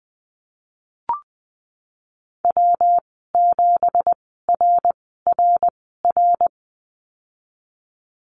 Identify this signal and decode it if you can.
Morse code, heard as a keyed tone, about 20 wpm, spaced as for 15 wpm, 700 Hz. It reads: W7RRR
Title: 193 Lesson 31g - Letter v - Call Signs 20wpm